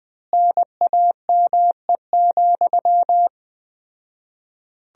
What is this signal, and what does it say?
Morse code, heard as a keyed tone, about 21 wpm, 700 Hz. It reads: NAME,
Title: 4x Word-Spacing - Sentences from Top 500 Words - 20wpm